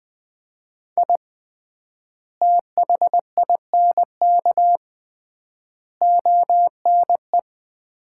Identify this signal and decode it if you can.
Morse code, heard as a keyed tone, about 21 wpm, 690 Hz. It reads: I THINK ONE
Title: 3x Word-Spacing - Sets of 5 Words - 20wpm